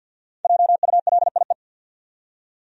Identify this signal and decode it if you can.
Morse code, heard as a keyed tone, about 50 wpm, 700 Hz. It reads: JULIE
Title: USA Common Names - Max 5 Letters - 50wpm